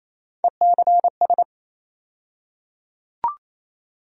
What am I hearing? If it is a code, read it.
Morse code, heard as a keyed tone, about 27 wpm, 700 Hz. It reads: ECS
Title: USA Call Sign Postfixes - 3 Letters 28wpm